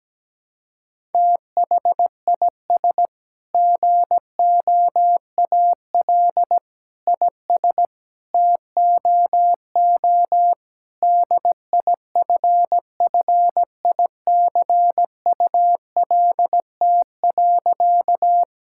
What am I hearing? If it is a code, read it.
Morse code, heard as a keyed tone, about 17 wpm, 700 Hz. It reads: THIS GOAL IS TOO DIFFICULT.
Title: Sentences from Top 1000 Words - Review Part 5 - 17wpm